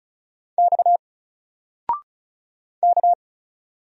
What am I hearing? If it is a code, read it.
Morse code, heard as a keyed tone, about 36 wpm, 700 Hz. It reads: X K
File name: Single-Letter-Number-35wpm